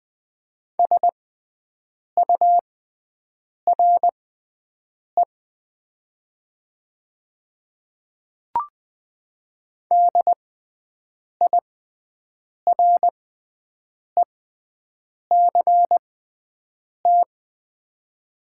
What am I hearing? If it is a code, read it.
Morse code, heard as a keyed tone, about 20 wpm, spaced as for 7 wpm, 700 Hz. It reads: SURE DIRECT
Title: 6x Character Spacing - Top 500 Words - 20wpm